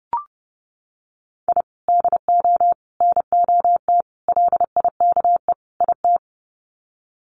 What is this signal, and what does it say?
Morse code, heard as a keyed tone, about 30 wpm, 700 Hz. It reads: I DO NOT LIKE IT
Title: Sets of 5 Words Encore 30wpm